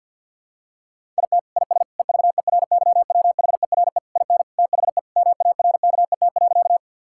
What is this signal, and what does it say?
Morse code, heard as a keyed tone, about 50 wpm, 700 Hz. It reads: IT IS EVERYWHERE IN THE MARKET.
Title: Sentences from Top 1600 Words 50wpm